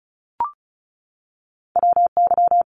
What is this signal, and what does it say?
Morse code, heard as a keyed tone, about 35 wpm, 700 Hz. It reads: WY
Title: ARRL Sections 35wpm